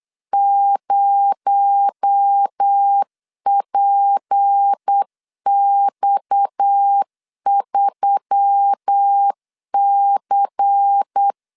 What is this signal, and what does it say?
Morse code, heard as a keyed tone, about 8 wpm, 800 Hz. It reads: 0PX3C
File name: LEZIONI / 6 PKX0 / 6 PKX0 8wpm 3341